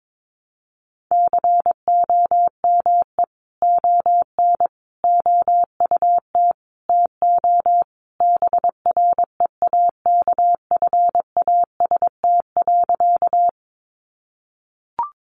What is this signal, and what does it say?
Morse code, heard as a keyed tone, about 22 wpm, 700 Hz. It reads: COME ON OUT TO BREAKFAST.
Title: Sentences from Top 1500 Words 22wpm